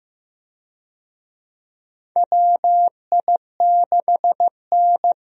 Morse code, heard as a keyed tone, about 15 wpm, 700 Hz. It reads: WI6N
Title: USA Call Signs - 4 Characters - Repeated 3x - 15wpm